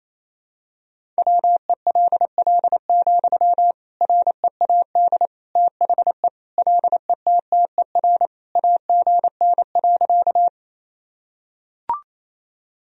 Morse code, heard as a keyed tone, about 28 wpm, 700 Hz. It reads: WELL, READ THE LETTER AGN.
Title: Sentences from Top 300 Words and Abbreviations 28wpm